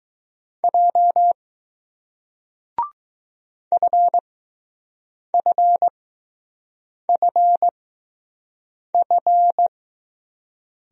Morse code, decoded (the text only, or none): J FFFF